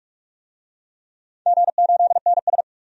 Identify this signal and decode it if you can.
Morse code, heard as a keyed tone, about 46 wpm, 700 Hz. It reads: G8NS
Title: DXCC-100 Call Signs - 4 Characters - 45wpm